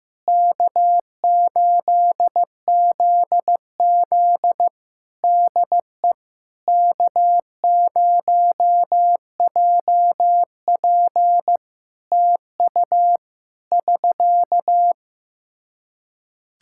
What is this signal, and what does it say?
Morse code, heard as a keyed tone, about 15 wpm, 700 Hz. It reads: K8ZZ DE K0JP TU <SK>
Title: Practice QSO - Exchange 4 - Ending - 15wpm